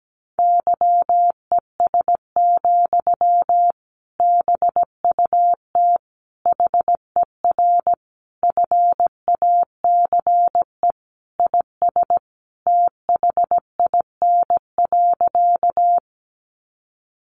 Morse code, heard as a keyed tone, about 17 wpm, 700 Hz. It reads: YES, BUT HER FACE IS THIN.
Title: Sentences from Top 1200 Words 17wpm